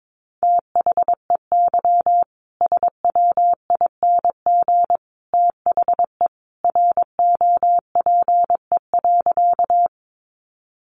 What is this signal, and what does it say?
Morse code, heard as a keyed tone, about 22 wpm, 700 Hz. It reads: THEY SWING THE ROPE.
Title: Easy Phrases 22wpm